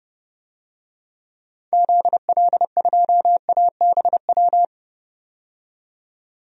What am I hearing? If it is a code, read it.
Morse code, heard as a keyed tone, about 30 wpm, 700 Hz. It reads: ZL2ABW
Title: DXCC-100 Call Signs - 6 Characters - 30wpm